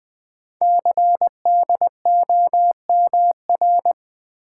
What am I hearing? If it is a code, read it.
Morse code, heard as a keyed tone, about 21 wpm, 700 Hz. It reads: CDOMR
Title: Code Groups - 5 Letters - 20wpm